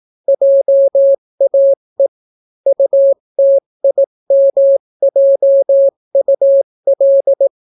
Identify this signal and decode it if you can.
Morse code, heard as a keyed tone, about 18 wpm, 550 Hz. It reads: JAE UTIMJUL